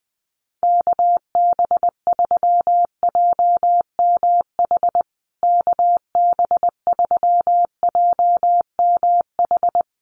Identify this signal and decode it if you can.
Morse code, heard as a keyed tone, about 20 wpm, 700 Hz. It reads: KB3JMH KB3JMH
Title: USA Call Signs Repeated 3x 20wpm